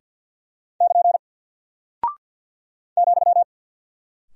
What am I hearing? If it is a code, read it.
Morse code, heard as a keyed tone, about 48 wpm, 700 Hz. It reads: <KN> ,
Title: Single Character (Letter:Number:Punctuation:Prosign) Alt 50wpm